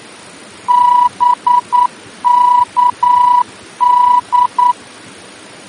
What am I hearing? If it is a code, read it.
Morse code, heard as a keyed tone, about 9 wpm, 980 Hz. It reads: BKD
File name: germany / mx / BKD-D-VOR